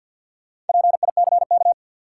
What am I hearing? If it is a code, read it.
Morse code, heard as a keyed tone, about 53 wpm, 700 Hz. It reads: PICK
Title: Sentences from Top 1000 Words - New Word Drill - 50wpm